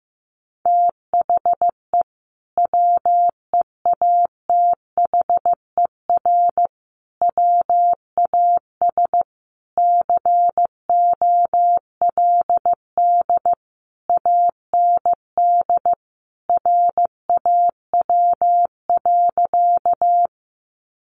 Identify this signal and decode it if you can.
Morse code, heard as a keyed tone, about 15 wpm, 700 Hz. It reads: THE WEATHER WAS COLD AND RAW.
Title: Sentences from Top 1500 Words 15wpm